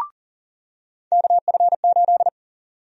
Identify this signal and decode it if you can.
Morse code, heard as a keyed tone, about 40 wpm, 700 Hz. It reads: KF8
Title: USA Call Sign Prefixes - 40wpm